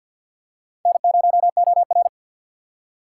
Morse code, heard as a keyed tone, about 52 wpm, 700 Hz. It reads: N0OR
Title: USA Call Signs - 4 Characters - 3x Character Spacing - 50wpm